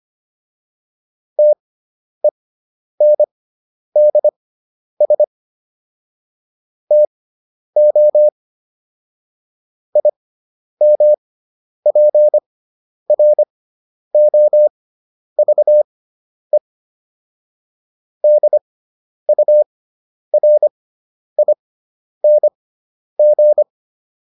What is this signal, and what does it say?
Morse code, heard as a keyed tone, about 25 wpm, spaced as for 10 wpm, 600 Hz. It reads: TENDS TO IMPROVE DURING